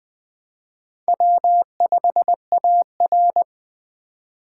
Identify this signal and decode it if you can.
Morse code, heard as a keyed tone, about 21 wpm, 700 Hz. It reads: W5AR
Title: Speed Racing - USA 1x2 Call Signs 17wpm